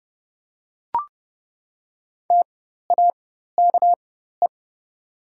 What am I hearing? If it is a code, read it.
Morse code, heard as a keyed tone, about 30 wpm, spaced as for 14 wpm, 700 Hz. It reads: TAKE
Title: 4x Character Spacing - Top 200 Words - 30wpm